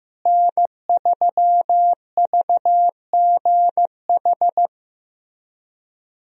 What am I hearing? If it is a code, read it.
Morse code, heard as a keyed tone, about 16 wpm, 700 Hz. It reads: N3VGH
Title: USA Call Signs - 5 Characters - Repeated 3x - 15wpm